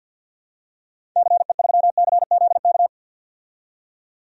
Morse code, heard as a keyed tone, about 48 wpm, 700 Hz. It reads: CE3CZK
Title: DXCC-100 Call Signs - 6 Characters - 50wpm